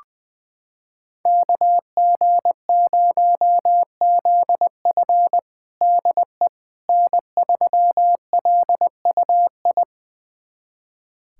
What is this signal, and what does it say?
Morse code, heard as a keyed tone, about 20 wpm, 700 Hz. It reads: KG0ZF DE N3LUI
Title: USA Callsign DE Callsign - 5 Characters - 20wpm